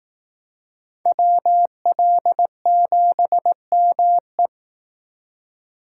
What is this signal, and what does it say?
Morse code, heard as a keyed tone, about 18 wpm, 700 Hz. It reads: WL7ME
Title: Speed Racing - USA 2x2 Call Signs 15wpm